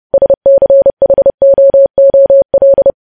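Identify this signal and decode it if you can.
Morse code, heard as a keyed tone, about 30 wpm, 550 Hz. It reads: SCHOOL
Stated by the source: News headlines sent in Morse code at 30.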